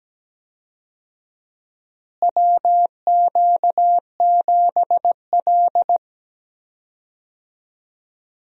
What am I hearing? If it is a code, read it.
Morse code, heard as a keyed tone, about 17 wpm, 700 Hz. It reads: WQ7L
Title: USA Call Signs - 4 Characters - Repeated 2x - 17wpm